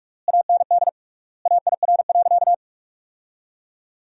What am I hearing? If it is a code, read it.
Morse code, heard as a keyed tone, about 46 wpm, 700 Hz. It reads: AND AIR.
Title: 3x Word-Spacing - Sentences from Top 400 Words - 45wpm